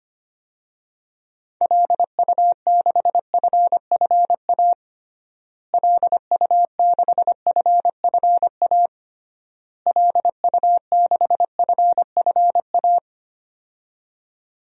Morse code, heard as a keyed tone, about 25 wpm, 700 Hz. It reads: LU6FFA LU6FFA LU6FFA
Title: DXCC-100 Call Signs - 6 Characters - Repeated 3x - 25wpm